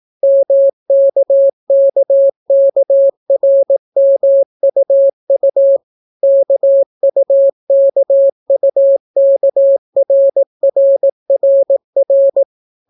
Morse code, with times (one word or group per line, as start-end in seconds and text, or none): 0.2-5.8 MKKKRMUU
6.2-12.4 KUKUKRRRR